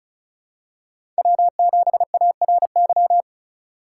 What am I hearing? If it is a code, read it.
Morse code, heard as a keyed tone, about 36 wpm, 700 Hz. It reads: W7ARY
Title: USA Call Signs - 5 Characters - 2x Character Spacing - 35wpm